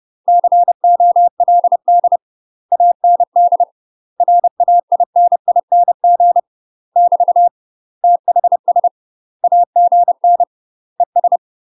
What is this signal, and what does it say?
Morse code, heard as a keyed tone, about 30 wpm, 700 Hz. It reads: COLD AND RAINING <BT> THS AGN ES
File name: UK MASTER QSOs TRAINING / 30WPM / UK MASTER QSOs 0040